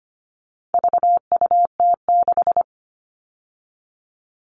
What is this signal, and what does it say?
Morse code, heard as a keyed tone, about 25 wpm, 700 Hz. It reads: VUT6
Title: All Cut Numbers - 4-Digit 25wpm